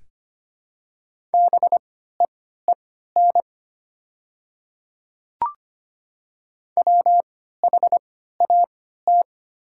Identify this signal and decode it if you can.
Morse code, heard as a keyed tone, about 25 wpm, spaced as for 14 wpm, 700 Hz. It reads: BEEN WHAT
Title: 3x Character Spacing - Top 200 Words - 25wpm